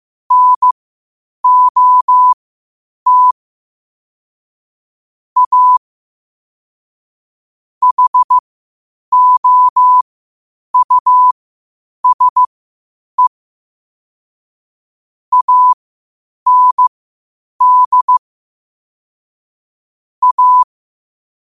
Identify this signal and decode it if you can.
Morse code, heard as a keyed tone, about 15 wpm, spaced as for 8 wpm, 1000 Hz. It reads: NOT A HOUSE AND A